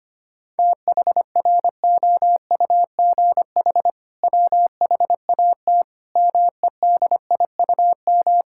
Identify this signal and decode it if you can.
Morse code, heard as a keyed tone, about 25 wpm, 700 Hz. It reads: THROUGH WHAT MEDIUM
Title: Sentences from Top 1000 Words - Review Part 6 - 25wpm